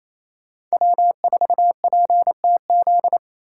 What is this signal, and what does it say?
Morse code, heard as a keyed tone, about 33 wpm, 700 Hz. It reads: W4PTZ
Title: Speed Racing - USA Call Signs 28wpm